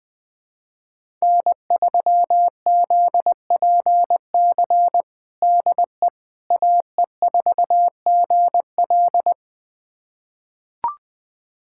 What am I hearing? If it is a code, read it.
Morse code, heard as a keyed tone, about 20 wpm, 700 Hz. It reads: N3ZPC DE AE4GL
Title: USA Callsign DE Callsign - 5 Characters - 20wpm